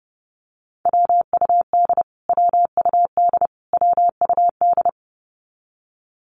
Morse code, heard as a keyed tone, about 30 wpm, 700 Hz. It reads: WUD WUD WUD
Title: Abbreviations Repeated 3x - 30wpm